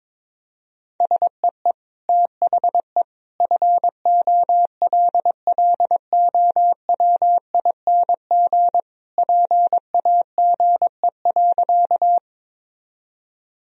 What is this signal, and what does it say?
Morse code, heard as a keyed tone, about 22 wpm, 700 Hz. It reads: SEE THE FOLLOWING PAGE.
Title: Sentences from Top 1000 Words - Review Part 1 - 22wpm